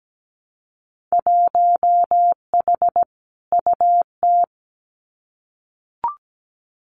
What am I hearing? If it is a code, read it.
Morse code, heard as a keyed tone, about 17 wpm, 700 Hz. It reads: 1H UT
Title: Winter Field Day Exchanges Encore 17wpm